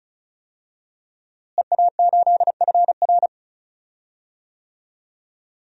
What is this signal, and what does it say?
Morse code, heard as a keyed tone, about 36 wpm, 700 Hz. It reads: EA8FR
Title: DXCC-100 Call Signs - 5 Characters - Repeated 3x - 35wpm